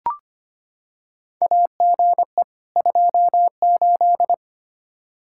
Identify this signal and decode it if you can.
Morse code, heard as a keyed tone, about 25 wpm, 700 Hz. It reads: AGE 28
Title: K1USN SST QSO Practice 25wpm